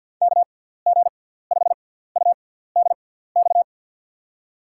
Morse code, heard as a keyed tone, about 46 wpm, spaced as for 17 wpm, 700 Hz. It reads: KG5UDX
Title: USA Call Signs - 6 Characters - 6x Character Spacing - 50wpm